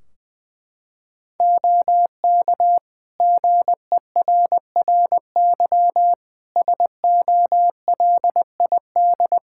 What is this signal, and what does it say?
Morse code, heard as a keyed tone, about 20 wpm, 700 Hz. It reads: OK GERRY SOLID
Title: Practice QSO - Exchange 3 - More Info - 20wpm